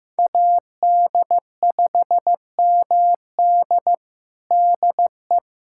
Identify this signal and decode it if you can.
Morse code, heard as a keyed tone, about 16 wpm, 700 Hz. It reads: AD5MD DE
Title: USA Callsign DE Callsign - 5 Characters - 15wpm